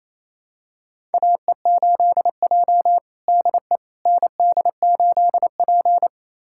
Morse code, heard as a keyed tone, about 28 wpm, 700 Hz. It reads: AE8J DE ND8P
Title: USA Callsign DE Callsign - 4 Characters - 28wpm